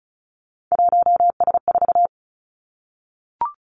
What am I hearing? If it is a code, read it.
Morse code, heard as a keyed tone, about 36 wpm, 700 Hz. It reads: 1S4
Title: Three Challenging Letter-Number Combinations 35wpm